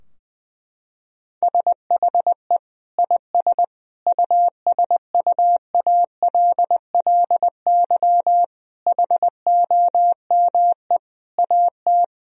Morse code, heard as a keyed tone, about 20 wpm, 700 Hz. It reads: SHE IS USUALLY HOME AT
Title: Sentences from Top 1000 Words - Review Part 3 - 20wpm